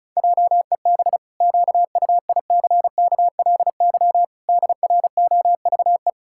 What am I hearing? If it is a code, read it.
Morse code, heard as a keyed tone, about 34 wpm, 700 Hz. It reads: JEB QUICKLY DROVE
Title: Pangram Challenge v2 - 35wpm